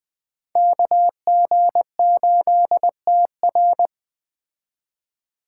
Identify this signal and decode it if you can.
Morse code, heard as a keyed tone, about 21 wpm, 700 Hz. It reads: KG8TR
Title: Speed Racing - USA 2x2 Call Signs 17wpm